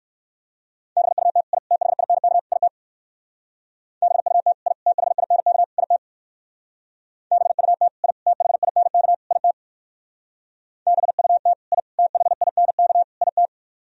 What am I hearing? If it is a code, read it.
Morse code, heard as a keyed tone, about 56 wpm, 700 Hz. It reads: BUT I THINK IT BUT I THINK IT BUT I THINK IT BUT I THINK IT
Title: Speed-Racing - Sets of 4 Words 45wpm